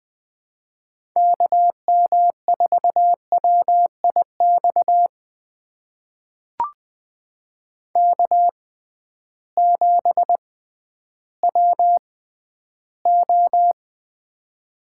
Morse code, heard as a keyed tone, about 20 wpm, spaced as for 12 wpm, 700 Hz. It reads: KM4WIX K7WO
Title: USA Call Signs - 6x Character Spacing - 20wpm